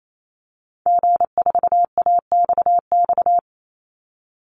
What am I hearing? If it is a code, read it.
Morse code, heard as a keyed tone, about 28 wpm, 700 Hz. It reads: G4AXX
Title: Call Signs - 28wpm